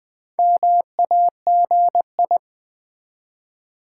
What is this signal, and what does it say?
Morse code, heard as a keyed tone, about 20 wpm, 700 Hz. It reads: MAGI
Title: Build A Word - 1 Letter At A Time - 5-Letter Words - 20wpm